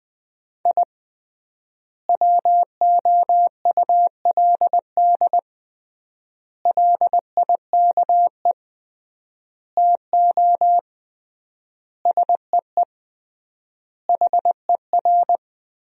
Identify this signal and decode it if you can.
Morse code, heard as a keyed tone, about 21 wpm, 700 Hz. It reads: I WOULD LIKE TO SEE HER
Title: 3x Word-Spacing - Sentences from Top 200 Words - 20wpm